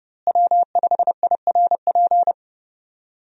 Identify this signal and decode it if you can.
Morse code, heard as a keyed tone, about 29 wpm, 700 Hz. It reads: W5IRP
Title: USA Call Signs - 5 Characters - Repeated 3x - 30wpm